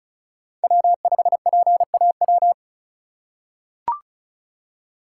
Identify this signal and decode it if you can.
Morse code, heard as a keyed tone, about 36 wpm, 700 Hz. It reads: W5PAW
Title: USA Call Signs - 5 Characters - 4x Character Spacing - 35wpm